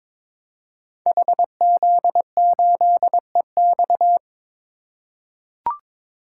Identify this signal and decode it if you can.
Morse code, heard as a keyed tone, about 22 wpm, 700 Hz. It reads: HZ8EX
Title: Code Groups - 5 Letter:Numbers - 22wpm